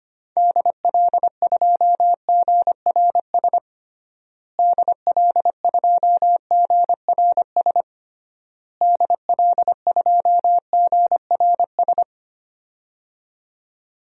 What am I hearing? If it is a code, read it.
Morse code, heard as a keyed tone, about 25 wpm, 700 Hz. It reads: DL2GRS DL2GRS DL2GRS
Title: DXCC-100 Call Signs - 6 Characters - Repeated 3x - 25wpm